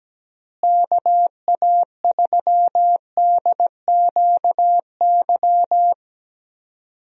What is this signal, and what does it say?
Morse code, heard as a keyed tone, about 17 wpm, 700 Hz. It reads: KA3DQY
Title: USA Call Signs - 6 Characters - Repeated 3x - 17wpm